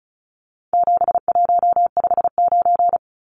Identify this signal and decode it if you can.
Morse code, heard as a keyed tone, about 34 wpm, 700 Hz. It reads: 7159
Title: Numbers - 4-Digit 35wpm